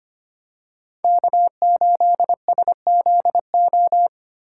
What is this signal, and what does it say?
Morse code, heard as a keyed tone, about 25 wpm, 700 Hz. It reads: K8SZO
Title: USA Call Signs - 5 Characters - 2x Character Spacing - 25wpm